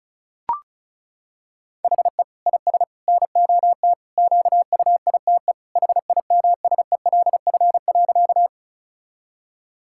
Morse code, heard as a keyed tone, about 34 wpm, 700 Hz. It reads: HE IS NOT QUITE HIMSELF.
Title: Sentences from Top 1600 Words 35wpm